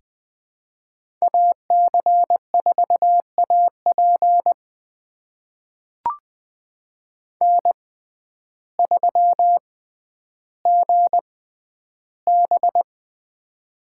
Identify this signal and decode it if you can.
Morse code, heard as a keyed tone, about 20 wpm, spaced as for 11 wpm, 700 Hz. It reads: AC4AP N3GB
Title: USA Call Signs - 6x Character Spacing - 20wpm